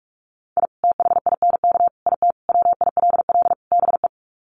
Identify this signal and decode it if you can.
Morse code, heard as a keyed tone, about 44 wpm, 700 Hz. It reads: I THINK IT WILL BE
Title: Sets of 5 Words Encore2 45wpm